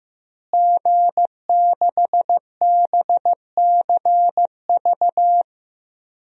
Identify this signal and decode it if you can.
Morse code, heard as a keyed tone, about 15 wpm, 700 Hz. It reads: G6BCV
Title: DXCC-100 Call Signs - 5 Characters - Repeated 3x - 15wpm